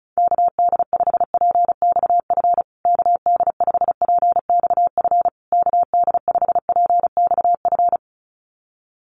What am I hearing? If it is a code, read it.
Morse code, heard as a keyed tone, about 34 wpm, 700 Hz. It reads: KD5PXF KD5PXF KD5PXF
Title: USA Call Signs Repeated 3x 35wpm